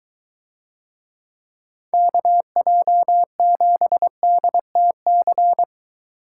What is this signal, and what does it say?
Morse code, heard as a keyed tone, about 21 wpm, 700 Hz. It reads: KJ7DTC